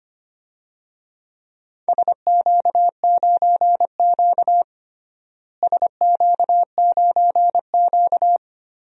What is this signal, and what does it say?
Morse code, heard as a keyed tone, about 25 wpm, 700 Hz. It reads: SQ9Q SQ9Q
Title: DXCC-100 Call Signs - 4 Characters - Repeated 3x - 25wpm